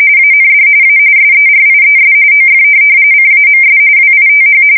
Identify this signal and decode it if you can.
RTTY, about 45.45 baud. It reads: AND A SERIES OF CAMS WITH
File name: fsk170 / morkrum